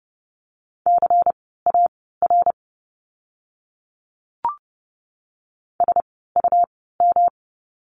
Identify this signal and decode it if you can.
Morse code, heard as a keyed tone, about 30 wpm, spaced as for 17 wpm, 700 Hz. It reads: CAR SUM
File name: course / 30:10wpm / 108 Lesson 18f - Letter m - Words 30wpm